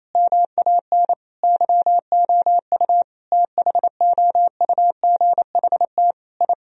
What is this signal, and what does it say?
Morse code, heard as a keyed tone, about 28 wpm, 700 Hz. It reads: MAN YOU THOUGHT I
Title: Sentences from Top 1000 Words - Review Part 3 - 28wpm